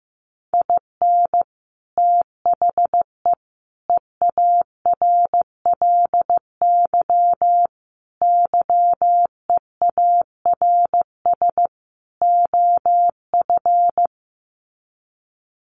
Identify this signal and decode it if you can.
Morse code, heard as a keyed tone, about 15 wpm, 700 Hz. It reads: IN THE EARLY YEARS OF
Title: Sets of 5 Words Encore 15wpm